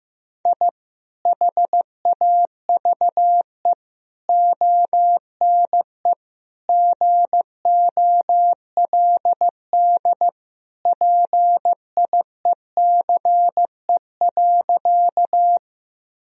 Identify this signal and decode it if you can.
Morse code, heard as a keyed tone, about 15 wpm, 700 Hz. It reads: I HAVE ONE GOLD PIECE.
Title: Sentences from Top 1000 Words - Review Part 3 - 15wpm